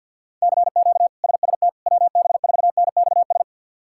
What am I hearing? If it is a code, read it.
Morse code, heard as a keyed tone, about 52 wpm, 700 Hz. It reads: CQ SST WB4NXS
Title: K1USN SST QSO Practice - 50wpm